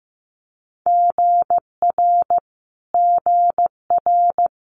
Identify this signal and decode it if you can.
Morse code, heard as a keyed tone, about 15 wpm, 700 Hz. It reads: GR GR
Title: Multi-letter Phonemes Repeated 3x 15wpm